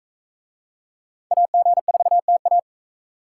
Morse code, heard as a keyed tone, about 41 wpm, 700 Hz. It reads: AG4TA